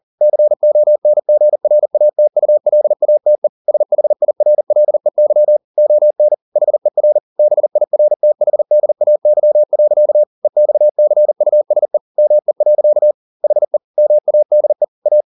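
Morse code, heard as a keyed tone, about 40 wpm, 600 Hz. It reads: CONGRATULATE SHIRLEY ON HER BIRTHDAY. EXCUSE ME. HE MADE A
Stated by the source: Morse code at 40 wpm, practice text, CW med Gnister.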